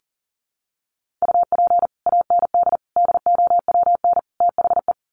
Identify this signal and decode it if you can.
Morse code, heard as a keyed tone, about 40 wpm, 700 Hz. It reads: UP AND DOWN THE
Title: Sets of 4 Words 40wpm